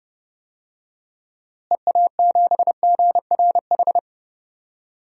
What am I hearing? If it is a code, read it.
Morse code, heard as a keyed tone, about 30 wpm, 700 Hz. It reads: EA7GRH
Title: DXCC-100 Call Signs - 6 Characters - Repeated 2x - 30wpm